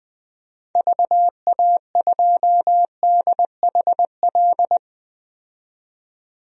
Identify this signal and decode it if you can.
Morse code, heard as a keyed tone, about 20 wpm, 700 Hz. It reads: VA2DHL
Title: DXCC-100 Call Signs - 6 Characters - 20wpm